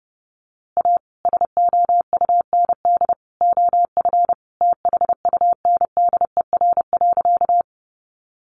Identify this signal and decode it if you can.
Morse code, heard as a keyed tone, about 30 wpm, 700 Hz. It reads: A SOUND OF THUNDER.
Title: A Sound Of Thunder 30wpm